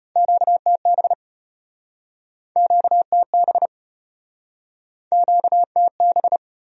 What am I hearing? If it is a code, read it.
Morse code, heard as a keyed tone, about 33 wpm, 700 Hz. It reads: QTB QTB QTB